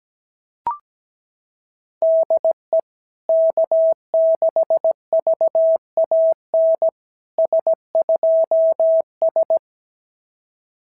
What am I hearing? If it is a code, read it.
Morse code, heard as a keyed tone, about 17 wpm, 650 Hz. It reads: DE K6VAN S2S
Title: SOTA QSO Practice 17wpm